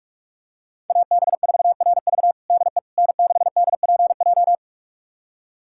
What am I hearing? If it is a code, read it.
Morse code, heard as a keyed tone, about 42 wpm, 700 Hz. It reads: AB4RV DE N6DPJ
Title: USA Callsign DE Callsign - 5 Characters - 45wpm